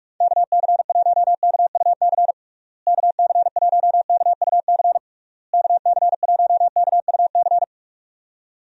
Morse code, heard as a keyed tone, about 43 wpm, 700 Hz. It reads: KC1KUC KC1KUC KC1KUC
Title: USA Call Signs - 6 Characters - Repeated 3x - 45wpm